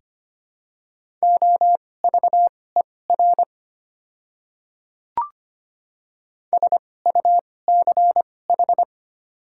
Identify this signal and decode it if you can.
Morse code, heard as a keyed tone, about 25 wpm, spaced as for 18 wpm, 700 Hz. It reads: OVER SUCH
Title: 2x Character Spacing - Top 300 Words - 25wpm